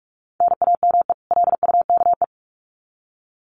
Morse code, heard as a keyed tone, about 44 wpm, 700 Hz. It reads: NAME LUKE
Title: Common QSO Phrases 45wpm